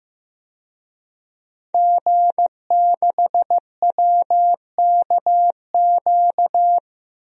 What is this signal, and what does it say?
Morse code, heard as a keyed tone, about 15 wpm, 700 Hz. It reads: G6WKQ
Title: DXCC-100 Call Signs - 5 Characters - 15wpm